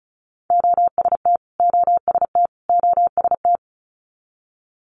Morse code, heard as a keyed tone, about 36 wpm, 700 Hz. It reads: OST OST OST
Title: Multi-letter Phonemes Repeated 3x 35wpm